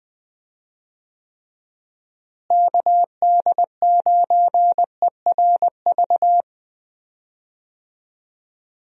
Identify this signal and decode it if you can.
Morse code, heard as a keyed tone, about 20 wpm, 700 Hz. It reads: KD9ERV
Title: USA Call Signs - 6 Characters - Repeated 3x - 20wpm